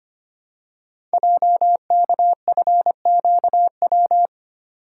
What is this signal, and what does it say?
Morse code, heard as a keyed tone, about 25 wpm, 700 Hz. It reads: JKFQW
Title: Code Groups - 5 Letters - 25wpm